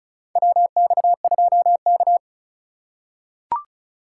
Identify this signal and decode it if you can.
Morse code, heard as a keyed tone, about 36 wpm, 700 Hz. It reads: WX2K
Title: USA Call Signs - 3x Character Spacing - 35wpm